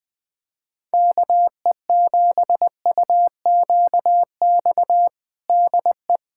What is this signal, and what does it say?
Morse code, heard as a keyed tone, about 20 wpm, 700 Hz. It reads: KE7UQX DE
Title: USA Callsign DE Callsign 20wpm